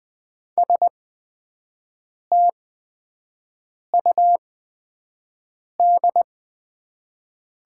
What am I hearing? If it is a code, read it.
Morse code, heard as a keyed tone, about 20 wpm, spaced as for 5 wpm, 700 Hz. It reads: STUD
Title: 8x Character Spacing - Top 300 Words - 20wpm